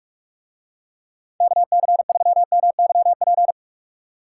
Morse code, heard as a keyed tone, about 44 wpm, 700 Hz. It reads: KC3MYP